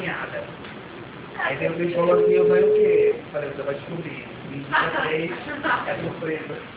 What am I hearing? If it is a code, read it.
Ringback tone (Europe, 425 Hz), with speech.